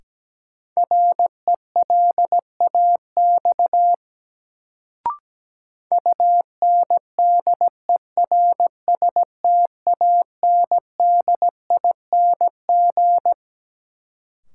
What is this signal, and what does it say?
Morse code, heard as a keyed tone, about 17 wpm, 700 Hz. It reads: RELAX UNDERSTANDING
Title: Sentences from Top 1400 Words - New Word Drill Alt - 17wpm